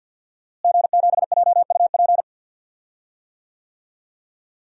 Morse code, heard as a keyed tone, about 46 wpm, 700 Hz. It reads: G7JUP
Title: DXCC-100 Call Signs - 5 Characters - Repeated 2x - 50wpm